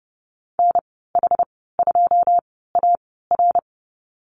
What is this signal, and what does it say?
Morse code, heard as a keyed tone, about 30 wpm, spaced as for 17 wpm, 700 Hz. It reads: NH2AR
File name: Lesson 15g - Number 5 - Call Signs 30wpm